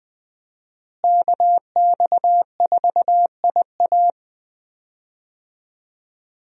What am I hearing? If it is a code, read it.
Morse code, heard as a keyed tone, about 20 wpm, 700 Hz. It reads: KX4IA
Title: USA Call Signs - 5 Characters - Repeated 3x - 20wpm